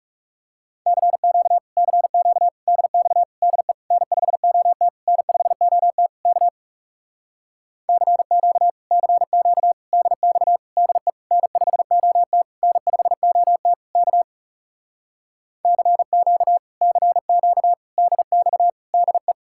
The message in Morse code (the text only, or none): CQ CQ DX DE N5OT N5OT K CQ CQ DX DE N5OT N5OT K CQ CQ DX DE